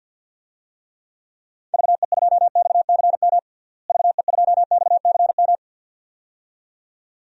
Morse code, heard as a keyed tone, about 53 wpm, 700 Hz. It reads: VE2XCM VE2XCM
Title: DXCC-100 Call Signs - 6 Characters - Repeated 2x - 50wpm